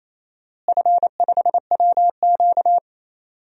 Morse code, heard as a keyed tone, about 27 wpm, 700 Hz. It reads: F5WQ
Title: DXCC-100 Call Signs - 4 Characters - Repeated 2x - 28wpm